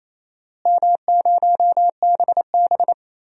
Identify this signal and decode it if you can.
Morse code, heard as a keyed tone, about 28 wpm, 700 Hz. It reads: M0BB
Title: DXCC-100 Call Signs - 4 Characters - Repeated 3x - 28wpm